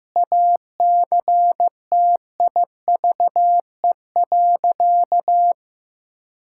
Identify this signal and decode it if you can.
Morse code, heard as a keyed tone, about 16 wpm, 700 Hz. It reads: ACTIVE.
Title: Sentences from Top 1000 Words - Review Part 6 - 15wpm